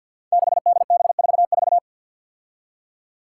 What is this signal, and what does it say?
Morse code, heard as a keyed tone, about 51 wpm, 700 Hz. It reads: 6DB44
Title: Five Challenging Letter-Number Combinations 50wpm